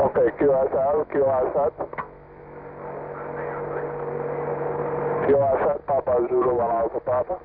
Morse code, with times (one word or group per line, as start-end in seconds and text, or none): none